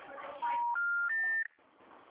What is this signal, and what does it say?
Special information tone, with noise.
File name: out- -2024- - - .14751